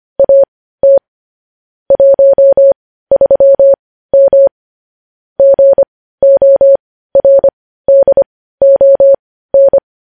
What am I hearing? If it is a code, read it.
Morse code, heard as a keyed tone, about 25 wpm, spaced as for 15 wpm, 550 Hz. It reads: AT 13M GORDON
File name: podcast / News headlines-1x-25x15